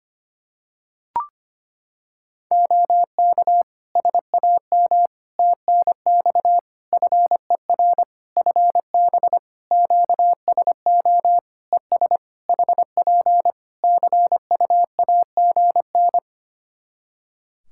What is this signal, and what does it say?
Morse code, heard as a keyed tone, about 25 wpm, 700 Hz. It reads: OK SAM TNX FER FB QSO ES HP CUAGN
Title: Practice QSO - Exchange 4 - Ending - 25wpm